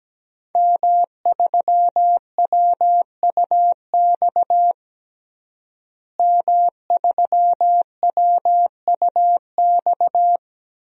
Morse code, heard as a keyed tone, about 17 wpm, 700 Hz. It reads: M3WUX M3WUX
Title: DXCC-100 Call Signs - 5 Characters - Repeated 3x - 17wpm